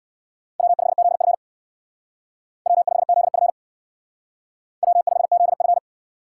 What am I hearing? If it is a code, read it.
Morse code, heard as a keyed tone, about 57 wpm, 700 Hz. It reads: W5ZF W5ZF W5ZF
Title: Speed Racing - USA 1x2 Call Signs 50wpm